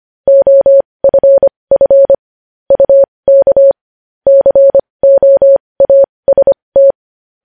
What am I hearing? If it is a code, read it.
Morse code, heard as a keyed tone, about 25 wpm, spaced as for 20 wpm, 550 Hz. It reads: OFF UK COAST
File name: podcast / News headlines-1x-25x20